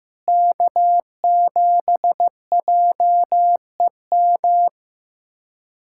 Morse code, heard as a keyed tone, about 15 wpm, 700 Hz. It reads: K7JEM
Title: K1USN SST QSO Practice - 15wpm